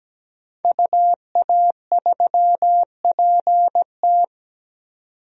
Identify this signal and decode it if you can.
Morse code, heard as a keyed tone, about 17 wpm, 700 Hz. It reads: UA3PT
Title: DXCC-100 Call Signs - 5 Characters - 17wpm